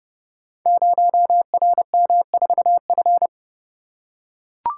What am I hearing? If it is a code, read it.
Morse code, heard as a keyed tone, about 30 wpm, 700 Hz. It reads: 0RM4F
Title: Code Groups - 5 Letter:Numbers - 30wpm